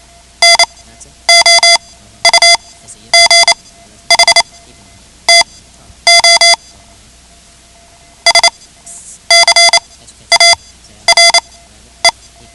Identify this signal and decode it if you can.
Morse code, heard as a keyed tone, about 28 wpm, spaced as for 11 wpm, 2050 Hz. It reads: NOUGH TO SCARE
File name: ham radio / CodeQuick / Code Quick 10